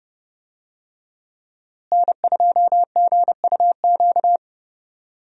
Morse code, heard as a keyed tone, about 32 wpm, 700 Hz. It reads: N2GUQ
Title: USA Call Signs - 5 Characters - Repeated 2x - 30wpm